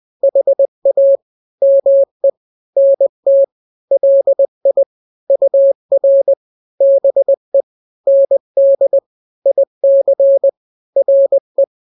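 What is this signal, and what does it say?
Morse code, heard as a keyed tone, about 20 wpm, 550 Hz. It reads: HA ME NT LI UR BE ND IC RE
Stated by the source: Morse drill of two-letter n-grams at 20/18 WPM (Farnsworth), from KK7RR.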